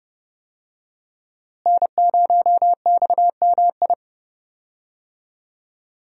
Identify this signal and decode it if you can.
Morse code, heard as a keyed tone, about 30 wpm, 700 Hz. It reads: N0XMI
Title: USA Call Signs - 5 Characters - Repeated 2x - 30wpm